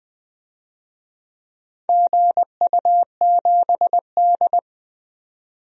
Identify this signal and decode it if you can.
Morse code, heard as a keyed tone, about 20 wpm, 700 Hz. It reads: GU7D
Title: DXCC-100 Call Signs - 4 Characters - 20wpm